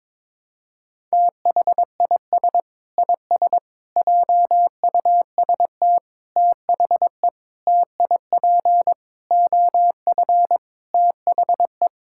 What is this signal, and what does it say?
Morse code, heard as a keyed tone, about 22 wpm, 700 Hz. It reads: THIS IS JUST THE TIP OF THE
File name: practice / 22wpm / Sentences with 8-Word Phrases - 22wpm